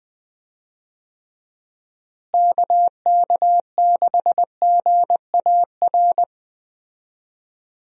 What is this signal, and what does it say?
Morse code, heard as a keyed tone, about 20 wpm, 700 Hz. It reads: KK6GAR
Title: USA Call Signs - 6 Characters - Repeated 2x - 20wpm